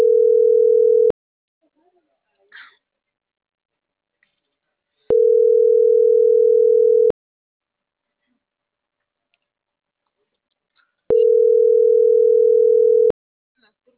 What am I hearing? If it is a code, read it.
Ringback tone (North America), 440 + 480 Hz.